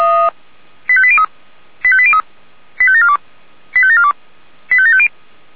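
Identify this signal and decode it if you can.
Five-tone selective call (ZVEI), heard as quick runs of five tones, tones of about 70 ms. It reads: 85803 85803 85742 85742 85750
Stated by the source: ZVEI five-tone call received by SDR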